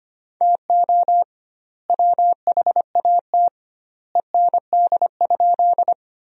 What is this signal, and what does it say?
Morse code, heard as a keyed tone, about 25 wpm, 700 Hz. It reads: TO WHAT END?
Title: 2x Word-Spacing - Sentences from Top 200 Words - 25wpm